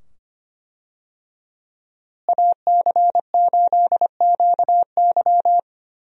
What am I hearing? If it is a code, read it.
Morse code, heard as a keyed tone, about 25 wpm, 700 Hz. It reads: AC8QY
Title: USA Call Signs - 5 Characters - Repeated 2x - 25wpm